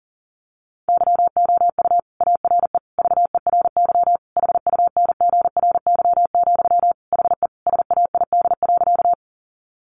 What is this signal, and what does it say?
Morse code, heard as a keyed tone, about 40 wpm, 700 Hz. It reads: YOU ARE VERY HUNGRY, HE SAID.
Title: Sentences from Top 1400 Words 40wpm